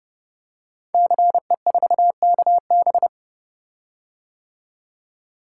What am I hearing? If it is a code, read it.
Morse code, heard as a keyed tone, about 30 wpm, 700 Hz. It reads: CE4KB
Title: Code Groups - 5 Letter:Numbers - 30wpm